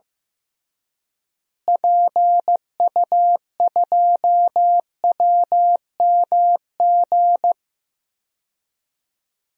Morse code, heard as a keyed tone, about 15 wpm, 700 Hz. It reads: PU2WMG
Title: DXCC-100 Call Signs - 6 Characters - Repeated 2x - 15wpm